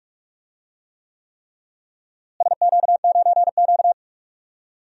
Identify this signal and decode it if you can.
Morse code, heard as a keyed tone, about 41 wpm, 690 Hz. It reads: SQ9Q
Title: DXCC-100 Call Signs - 4 Characters - Repeated 3x - 45wpm